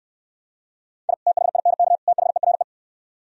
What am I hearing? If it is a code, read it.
Morse code, heard as a keyed tone, about 67 wpm, 700 Hz. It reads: I THINK THERE